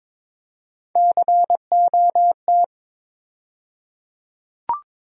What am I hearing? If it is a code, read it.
Morse code, heard as a keyed tone, about 22 wpm, 700 Hz. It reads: COT
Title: USA Call Sign Postfixes - 3 Letters 22wpm